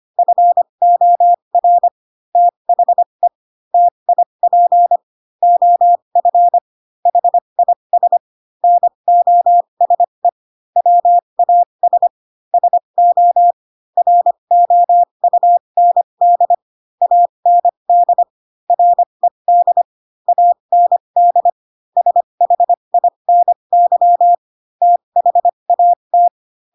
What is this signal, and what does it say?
Morse code, heard as a keyed tone, about 25 wpm, spaced as for 22 wpm, 700 Hz. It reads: FOR THE TIP OF HIS NOSE WAS SO ROUND AND RED AND SHINY THAT